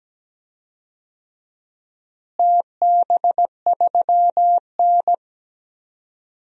Morse code, heard as a keyed tone, about 17 wpm, 700 Hz. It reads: TB3N
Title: DXCC-100 Call Signs - 4 Characters - Repeated 2x - 17wpm